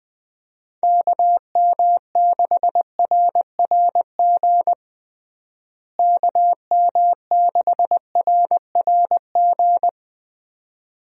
Morse code, heard as a keyed tone, about 21 wpm, 700 Hz. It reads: KM6RRG KM6RRG
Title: USA Call Signs - 6 Characters - Repeated 2x - 20wpm